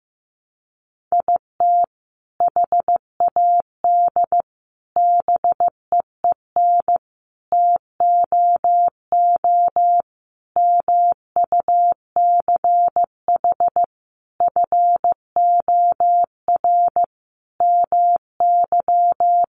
Morse code, heard as a keyed tone, about 15 wpm, 700 Hz. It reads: IT HAD BEEN TOO MUCH FOR MY
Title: Sentences from Top 1300 Words 15wpm